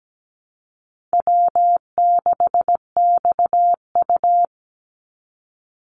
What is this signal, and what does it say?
Morse code, heard as a keyed tone, about 17 wpm, 700 Hz. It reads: W6XU
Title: Easy Call Signs - 17wpm